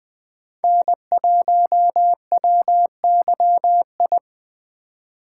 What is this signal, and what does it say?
Morse code, heard as a keyed tone, about 24 wpm, 700 Hz. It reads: N1WYI
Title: Speed Racing - USA Call Signs 20wpm